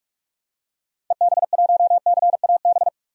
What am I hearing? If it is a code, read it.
Morse code, heard as a keyed tone, about 42 wpm, 700 Hz. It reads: EB1CAB
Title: DXCC-100 Call Signs - 6 Characters - 45wpm